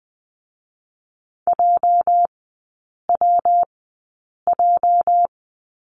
Morse code, heard as a keyed tone, about 20 wpm, 700 Hz. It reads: J W J
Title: 205 Lesson 33e - Letter j - Three Characters 20wpm